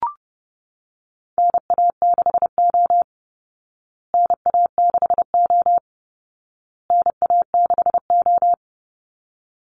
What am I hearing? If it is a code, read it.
Morse code, heard as a keyed tone, about 30 wpm, 700 Hz. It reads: NA6O NA6O NA6O
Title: Call Signs Repeated 2x - 30wpm